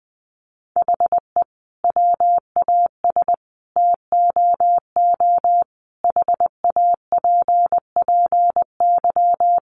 Morse code, heard as a keyed tone, about 20 wpm, 700 Hz. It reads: HE WAS TOO HAPPY
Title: Sentences from Top 1000 Words 20wpm